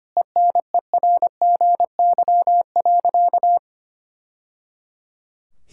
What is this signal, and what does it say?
Morse code, heard as a keyed tone, about 25 wpm, 700 Hz. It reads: ENERGY.
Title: Sentences from Top 1000 Words - Review Part 6 - 25wpm